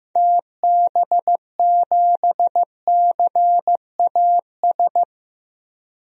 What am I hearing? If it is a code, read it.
Morse code, heard as a keyed tone, about 15 wpm, 700 Hz. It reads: TB7CAS
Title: DXCC-100 Call Signs - 6 Characters - 15wpm